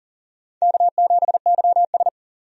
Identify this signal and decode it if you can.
Morse code, heard as a keyed tone, about 39 wpm, 700 Hz. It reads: K7YS
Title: USA Call Signs - 4 Characters - 40wpm